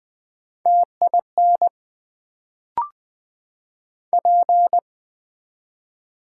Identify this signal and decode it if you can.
Morse code, heard as a keyed tone, about 20 wpm, 700 Hz. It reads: TIN P
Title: Build A Word - 1 Letter At A Time - 3-Letter Words - 20wpm